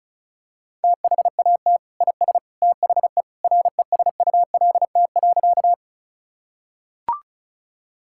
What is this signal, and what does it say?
Morse code, heard as a keyed tone, about 34 wpm, 700 Hz. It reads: THAT IS THE RESULT.
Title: Sentences from Top 1000 Words - Review Part 5 - 35wpm